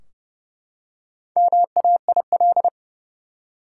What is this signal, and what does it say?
Morse code, heard as a keyed tone, about 32 wpm, 700 Hz. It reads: MAIL
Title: Four Letter Words - 30wpm